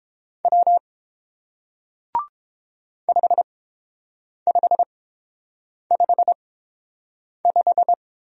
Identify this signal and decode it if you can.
Morse code, heard as a keyed tone, about 30 wpm, spaced as for 7 wpm, 700 Hz. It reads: W 5555